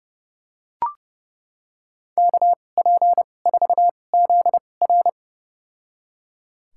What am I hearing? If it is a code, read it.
Morse code, heard as a keyed tone, about 30 wpm, spaced as for 22 wpm, 700 Hz. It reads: KP4ZR
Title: USA Call Signs - 5 Characters - 2x Character Spacing - 30wpm